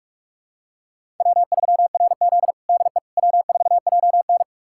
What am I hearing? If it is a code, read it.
Morse code, heard as a keyed tone, about 46 wpm, 700 Hz. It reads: W3RZ DE W4JN
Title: USA Callsign DE Callsign - 4 Characters - 45wpm